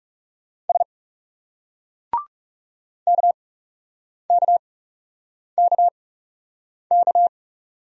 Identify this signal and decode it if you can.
Morse code, heard as a keyed tone, about 40 wpm, 700 Hz. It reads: S K K K K